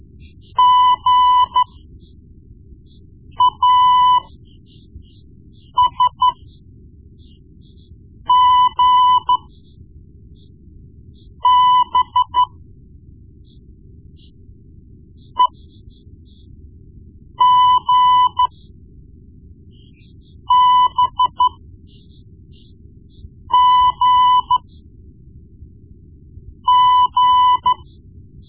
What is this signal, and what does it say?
Morse code, heard as a keyed tone, about 13 wpm, spaced as for 4 wpm, 1000 Hz. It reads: GASGB EGBGG
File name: Morse Lessons / Morse Lessons / Tape-1B